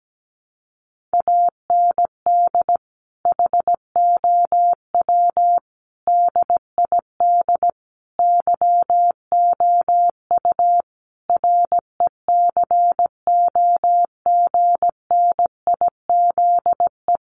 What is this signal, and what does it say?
Morse code, heard as a keyed tone, about 17 wpm, 700 Hz. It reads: AND HOW DID YOU RECOGNIZE
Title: Sentences from Top 1400 Words 17wpm